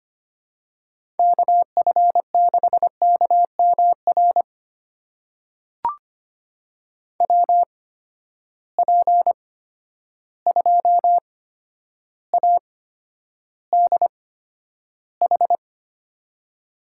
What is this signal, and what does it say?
Morse code, heard as a keyed tone, about 25 wpm, spaced as for 11 wpm, 700 Hz. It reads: KF6KMR WP2ADH
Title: USA Call Signs - 6 Characters - 8x Character Spacing - 25wpm